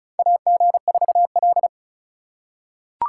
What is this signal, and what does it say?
Morse code, heard as a keyed tone, about 34 wpm, 700 Hz. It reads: AG4L
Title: USA Call Signs - 8x Character Spacing - 35wpm